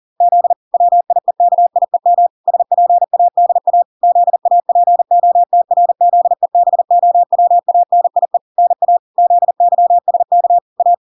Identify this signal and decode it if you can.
Morse code, heard as a keyed tone, about 40 wpm, 700 Hz. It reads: Z WIEKIEM SPADA ZAPOTRZEBOWANIE NA ZYSK A